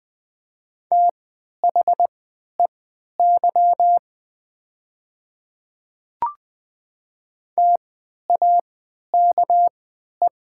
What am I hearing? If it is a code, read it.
Morse code, heard as a keyed tone, about 20 wpm, spaced as for 11 wpm, 700 Hz. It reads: THEY TAKE
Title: 3x Character Spacing - Top 200 Words - 20wpm